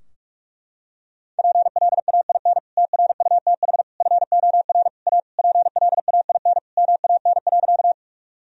Morse code, heard as a keyed tone, about 44 wpm, 700 Hz. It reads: PLAIN TRUTH FOR A PLAIN MAN.
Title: Sentences from Top 1000 Words - Review Part 2 - 45wpm